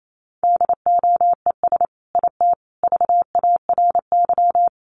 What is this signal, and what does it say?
Morse code, heard as a keyed tone, about 28 wpm, 700 Hz. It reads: DOES IT VARY
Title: Sentences from Top 1200 Words 28wpm